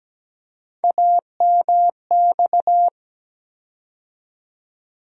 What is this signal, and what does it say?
Morse code, heard as a keyed tone, about 17 wpm, 700 Hz. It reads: AMX
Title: USA Call Sign Postfixes - 3 Letters 17wpm